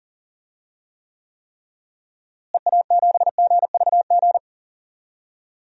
Morse code, heard as a keyed tone, about 40 wpm, 700 Hz. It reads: EA7GVG
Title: DXCC-100 Call Signs - 6 Characters - Repeated 3x - 40wpm